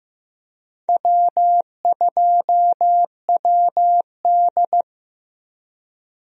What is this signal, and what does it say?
Morse code, heard as a keyed tone, about 15 wpm, 700 Hz. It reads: W2WD
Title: USA Call Signs - 4 Characters - Repeated 2x - 15wpm